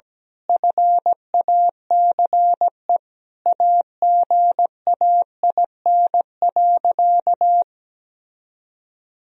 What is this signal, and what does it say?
Morse code, heard as a keyed tone, about 17 wpm, 700 Hz. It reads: FACE AGAIN.
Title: Sentences from Top 1000 Words - Review Part 2 - 17wpm